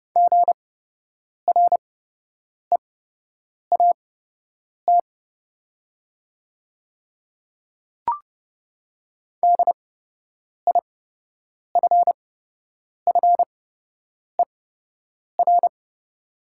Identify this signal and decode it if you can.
Morse code, heard as a keyed tone, about 30 wpm, spaced as for 8 wpm, 700 Hz. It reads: GREAT DIFFER
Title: 8x Character Spacing - Top 200 Words - 30wpm